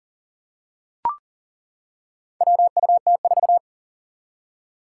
Morse code, heard as a keyed tone, about 40 wpm, 700 Hz. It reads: WUT4
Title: Four Challenging Letter-Number Combinations 40wpm